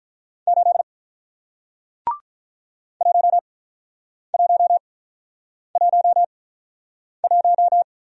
Morse code, heard as a keyed tone, about 48 wpm, spaced as for 9 wpm, 700 Hz. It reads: 8 1111